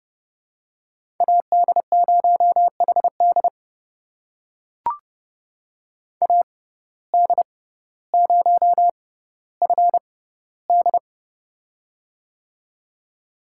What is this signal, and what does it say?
Morse code, heard as a keyed tone, about 30 wpm, spaced as for 15 wpm, 700 Hz. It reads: AD0HD AD0FD
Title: USA Call Signs - 5 Characters - 6x Character Spacing - 30wpm